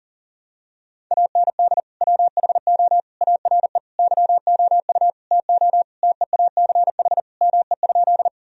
Morse code, heard as a keyed tone, about 40 wpm, 700 Hz. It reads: AND WHO ARE YOU TO TEACH ME?
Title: Sentences from Top 1000 Words - Review Part 2 - 40wpm